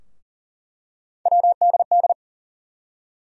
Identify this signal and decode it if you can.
Morse code, heard as a keyed tone, about 40 wpm, 700 Hz. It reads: WDD
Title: USA Call Sign Postfixes - 3 Letters 40wpm